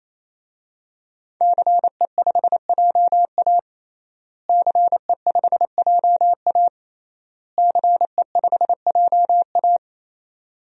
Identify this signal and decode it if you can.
Morse code, heard as a keyed tone, about 28 wpm, 700 Hz. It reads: CE5JA CE5JA CE5JA
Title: DXCC-100 Call Signs - 5 Characters - Repeated 3x - 28wpm